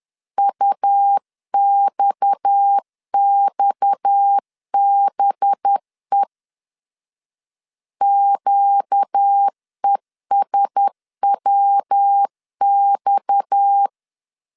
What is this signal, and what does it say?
Morse code, heard as a keyed tone, about 10 wpm, 800 Hz. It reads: UXXBE QESWX
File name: LEZIONI / 6 PKX0 / 6 PKX 10wpm 2426